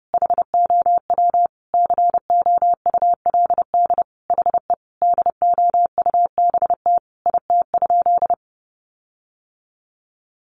Morse code, heard as a keyed tone, about 30 wpm, 700 Hz. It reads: HOW COULD HE DOUBT IT?
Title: Sentences from Top 1500 Words 30wpm